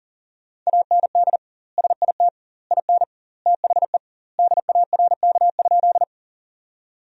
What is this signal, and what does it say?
Morse code, heard as a keyed tone, about 37 wpm, 700 Hz. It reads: AND SIT IN THE DARK?
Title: 2x Word-Spacing - Sentences from Top 500 Words - 40wpm